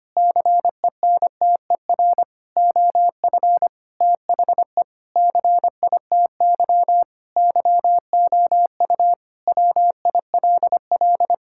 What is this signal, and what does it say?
Morse code, heard as a keyed tone, about 25 wpm, 700 Hz. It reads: CENTER OF THE CITY YOU WILL
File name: practice / 25wpm / Sentences with 4-Word Phrases - 25wpm